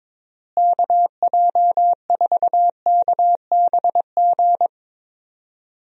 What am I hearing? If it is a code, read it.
Morse code, heard as a keyed tone, about 21 wpm, 700 Hz. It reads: KJ4KBG
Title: USA Call Signs - 6 Characters - Repeated 2x - 22wpm